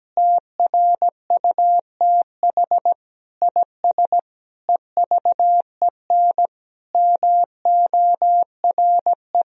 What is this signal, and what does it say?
Morse code, heard as a keyed tone, about 17 wpm, 700 Hz. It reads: TRUTH IS EVEN MORE
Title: Sentences from Top 1000 Words - Review Part 3 - 17wpm